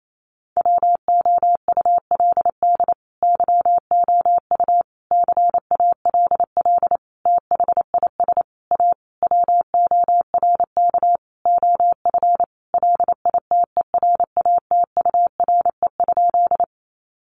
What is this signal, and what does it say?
Morse code, heard as a keyed tone, about 28 wpm, 700 Hz. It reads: WOULD YOU CALL THIS A WORK OF LITERATURE?
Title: Sentences from Top 600 Words 28wpm